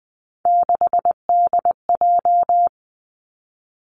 Morse code, heard as a keyed tone, about 20 wpm, 700 Hz. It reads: 6DJ
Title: Three Challenging Letter-Number Combinations 20wpm